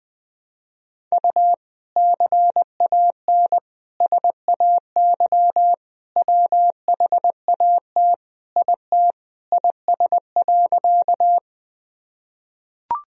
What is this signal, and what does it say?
Morse code, heard as a keyed tone, about 21 wpm, 700 Hz. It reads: U CAN SAY WHAT IT IS.
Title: Sentences from Top 200 Words and Abbreviations 20wpm